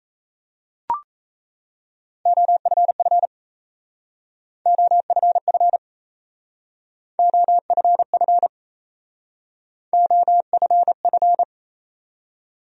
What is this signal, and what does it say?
Morse code, heard as a keyed tone, about 36 wpm, 700 Hz. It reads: OFF OFF OFF OFF